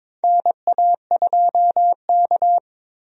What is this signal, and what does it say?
Morse code, heard as a keyed tone, about 22 wpm, 700 Hz. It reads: NA2K
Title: USA Call Signs - 4 Characters - Repeated 2x - 22wpm